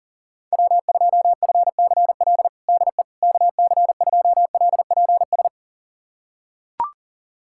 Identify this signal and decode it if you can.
Morse code, heard as a keyed tone, about 39 wpm, 700 Hz. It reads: W2FCL DE KC2LPS
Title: USA Callsign DE Callsign 40wpm